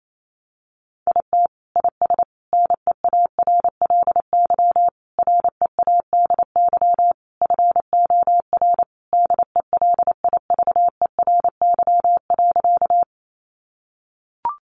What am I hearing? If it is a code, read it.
Morse code, heard as a keyed tone, about 28 wpm, 700 Hz. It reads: IT IS NEARLY READY FOR DELIVERY.
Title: Sentences from Top 1500 Words 28wpm